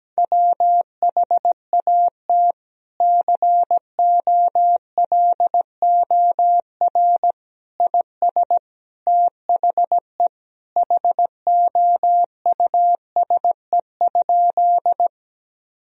Morse code, heard as a keyed tone, about 17 wpm, 700 Hz. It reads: WHAT COLOR IS THE HOUSE?
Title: Sentences from Top 1000 Words - Review Part 2 - 17wpm